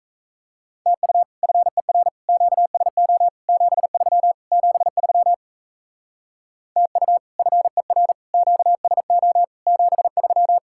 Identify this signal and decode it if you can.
Morse code, heard as a keyed tone, about 34 wpm, 700 Hz. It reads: TU FER QSO 73 73 TU FER QSO 73
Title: Speed-Racing Common QSO Phrases Encore - 28wpm